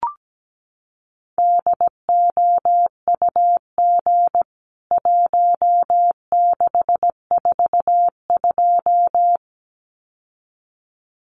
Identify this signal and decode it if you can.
Morse code, heard as a keyed tone, about 17 wpm, 700 Hz. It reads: DOUG 1642
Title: CWT Exchange 17wpm